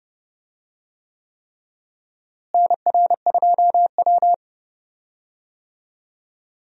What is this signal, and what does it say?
Morse code, heard as a keyed tone, about 32 wpm, 700 Hz. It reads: NR2W
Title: USA Call Signs - 4 Characters - Repeated 3x - 30wpm